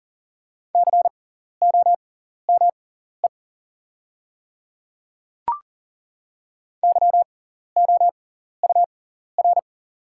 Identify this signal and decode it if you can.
Morse code, heard as a keyed tone, about 40 wpm, spaced as for 14 wpm, 700 Hz. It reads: COME YOUR
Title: 6x Character Spacing - Top 300 Words - 40wpm